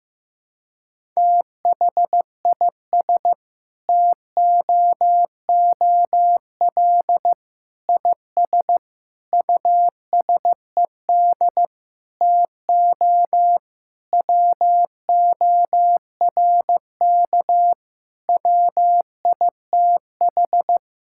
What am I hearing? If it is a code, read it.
Morse code, heard as a keyed tone, about 15 wpm, 700 Hz. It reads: THIS TOOL IS USED TO WORK WITH
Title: Sentences from Top 1000 Words - Review Part 6 - 15wpm